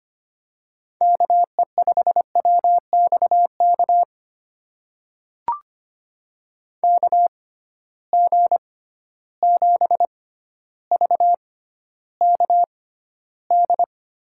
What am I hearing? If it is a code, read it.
Morse code, heard as a keyed tone, about 25 wpm, spaced as for 13 wpm, 700 Hz. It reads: KE5WXK KG7VKD
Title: USA Call Signs - 6 Characters - 6x Character Spacing - 25wpm